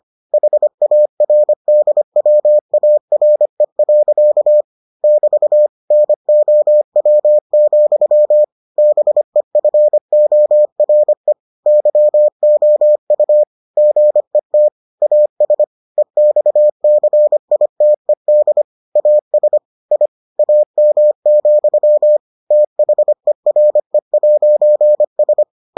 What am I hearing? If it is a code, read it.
Morse code, heard as a keyed tone, about 25 wpm, 600 Hz. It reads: HARDWARE. <BT> NOW, BEFORE YOU GET AS EXCITED AS I AM, THERE'S